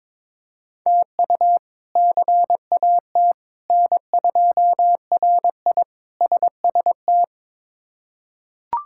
Morse code, heard as a keyed tone, about 21 wpm, 700 Hz. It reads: TU CAT N2RI SST
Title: K1USN SST QSO Practice - 22wpm